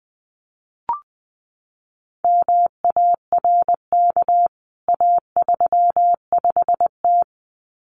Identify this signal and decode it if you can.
Morse code, heard as a keyed tone, about 20 wpm, 700 Hz. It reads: MARK A35T
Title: CWT Exchange 20wpm